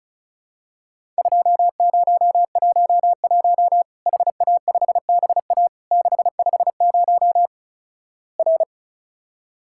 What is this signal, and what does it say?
Morse code, heard as a keyed tone, about 36 wpm, 700 Hz. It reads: 2011 HA5BA 650 R
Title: Worked All Europe DX Contest - QTC Exchange - 35wpm